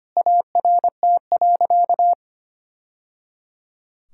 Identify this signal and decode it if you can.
Morse code, heard as a keyed tone, about 25 wpm, 700 Hz. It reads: ART.
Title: Sentences from Top 1000 Words - Review Part 6 - 25wpm